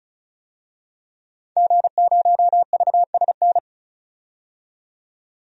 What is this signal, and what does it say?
Morse code, heard as a keyed tone, about 36 wpm, 700 Hz. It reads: G0VSN
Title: DXCC-100 Call Signs - 5 Characters - 35wpm